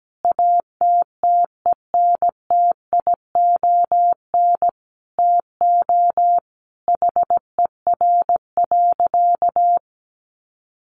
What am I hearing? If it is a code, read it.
Morse code, heard as a keyed tone, about 17 wpm, 700 Hz. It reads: ATTENTION TO HER.
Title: Sentences from Top 1000 Words 17wpm